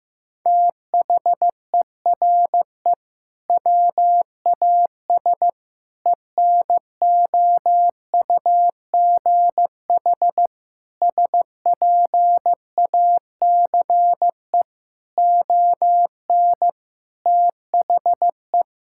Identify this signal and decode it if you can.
Morse code, heard as a keyed tone, about 15 wpm, 700 Hz. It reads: THERE WAS ENOUGH SPACE ON THE
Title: Sentences from Top 1000 Words - Review Part 6 - 15wpm